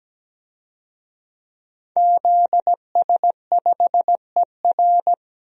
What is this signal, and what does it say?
Morse code, heard as a keyed tone, about 17 wpm, 700 Hz. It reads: ZS5ER
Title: DXCC-100 Call Signs - 5 Characters - Repeated 3x - 17wpm